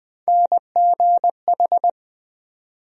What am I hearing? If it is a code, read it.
Morse code, heard as a keyed tone, about 20 wpm, 700 Hz. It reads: NGH
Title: USA Call Sign Postfixes - 3 Letters 20wpm